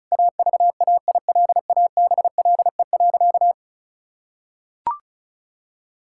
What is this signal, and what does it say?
Morse code, heard as a keyed tone, about 36 wpm, 700 Hz. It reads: AVAILABLE.
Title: Sentences from Top 1000 Words - Review Part 6 - 35wpm